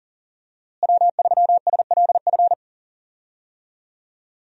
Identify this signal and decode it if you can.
Morse code, heard as a keyed tone, about 40 wpm, 700 Hz. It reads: W3SLF
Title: USA Call Signs - 5 Characters - Repeated 3x - 40wpm